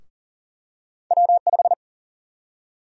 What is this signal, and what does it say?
Morse code, heard as a keyed tone, about 40 wpm, 700 Hz. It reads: W5
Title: USA Call Sign Prefixes - Repeated 2x - 40wpm